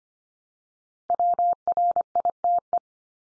Morse code, heard as a keyed tone, about 25 wpm, 700 Hz. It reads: WRITE
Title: Top 100 Words 25wpm x3